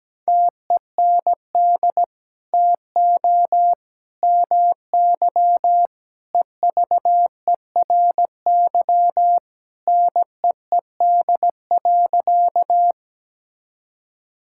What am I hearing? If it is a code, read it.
Morse code, heard as a keyed tone, about 17 wpm, 700 Hz. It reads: TEND TO MY EVERY NEED.
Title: Sentences from Top 1000 Words - Review Part 5 - 17wpm